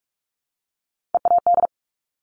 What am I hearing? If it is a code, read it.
Morse code, heard as a keyed tone, about 46 wpm, 700 Hz. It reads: EAD
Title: Top Three Letter Combinations in Words 45wpm